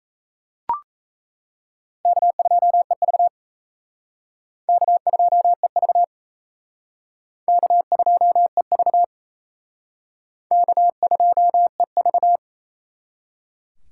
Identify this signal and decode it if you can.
Morse code, heard as a keyed tone, about 35 wpm, 700 Hz. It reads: K2EV K2EV K2EV K2EV
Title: Speed Racing - USA 1x2 Call Signs 28wpm